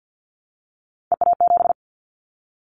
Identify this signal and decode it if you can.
Morse code, heard as a keyed tone, about 50 wpm, 700 Hz. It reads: EA7
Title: Callsign Prefixes for Top 25 DXCC - 50wpm